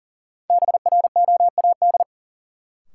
Morse code, heard as a keyed tone, about 37 wpm, 700 Hz. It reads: BROAD
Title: Sentences from Top 1000 Words - New Word Drill - 40wpm